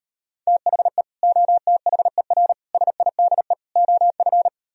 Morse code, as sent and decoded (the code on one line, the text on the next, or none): - .... / --- - .... .-. / ... .. -.. / --- ..-.
THE OTHER SIDE OF